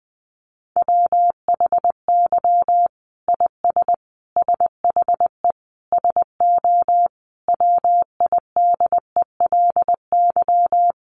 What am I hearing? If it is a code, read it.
Morse code, heard as a keyed tone, about 20 wpm, 700 Hz. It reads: WHY IS SHE SO WIDELY
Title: Sentences from Top 1200 Words 20wpm